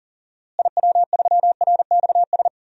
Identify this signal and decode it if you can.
Morse code, heard as a keyed tone, about 39 wpm, 700 Hz. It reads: IW3RXS
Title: DXCC-100 Call Signs - 6 Characters - Repeated 3x - 40wpm